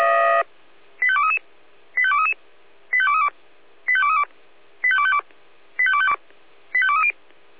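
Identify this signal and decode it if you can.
Five-tone selective call (ZVEI), heard as quick runs of five tones, tones of about 70 ms. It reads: 85233 85233 85232 85232 85252 85230
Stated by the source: ZVEI five-tone selective call, address 85885, received by SDR.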